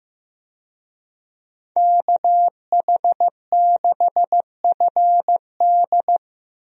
Morse code, heard as a keyed tone, about 15 wpm, 700 Hz. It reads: KH6FD
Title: USA Call Signs - 5 Characters - 15wpm